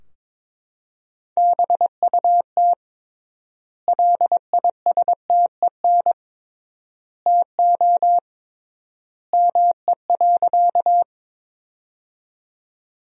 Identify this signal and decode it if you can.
Morse code, heard as a keyed tone, about 21 wpm, 700 Hz. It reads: BUT LISTEN TO ME.
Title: 3x Word-Spacing - Sentences from Top 400 Words - 22wpm